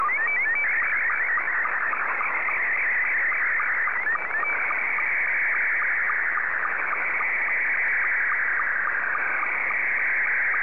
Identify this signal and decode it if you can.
SSTV (slow-scan television), Scottie 2.